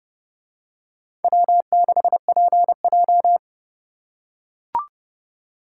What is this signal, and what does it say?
Morse code, heard as a keyed tone, about 27 wpm, 700 Hz. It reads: W6PJ